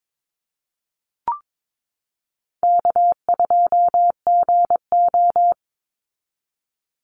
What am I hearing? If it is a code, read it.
Morse code, heard as a keyed tone, about 22 wpm, 690 Hz. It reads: K2GO
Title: Easy Call Signs Repeated 2x - 22wpm